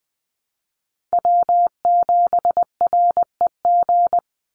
Morse code, heard as a keyed tone, about 20 wpm, 700 Hz. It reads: W7REG
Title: USA Call Signs 20wpm